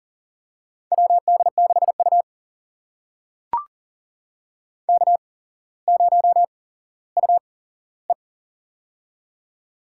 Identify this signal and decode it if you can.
Morse code, heard as a keyed tone, about 40 wpm, spaced as for 17 wpm, 700 Hz. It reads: WD6U K0UE
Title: USA Call Signs - 4 Characters - 8x Character Spacing - 40wpm